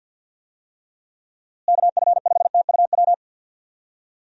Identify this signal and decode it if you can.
Morse code, heard as a keyed tone, about 47 wpm, 700 Hz. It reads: KF5TUW
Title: USA Call Signs - 6 Characters - Repeated 3x - 50wpm